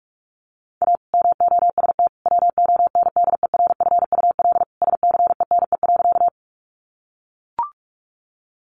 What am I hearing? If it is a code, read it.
Morse code, heard as a keyed tone, about 44 wpm, 700 Hz. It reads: A MOST WONDERFUL SCENE.
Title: Sentences from Top 1100 Words 45wpm